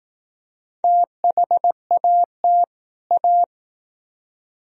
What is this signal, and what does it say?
Morse code, heard as a keyed tone, about 19 wpm, 700 Hz. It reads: THAT A